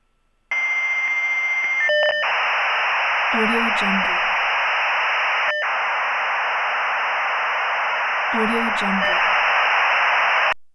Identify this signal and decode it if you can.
Dial-up modem handshake, with speech.
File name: preview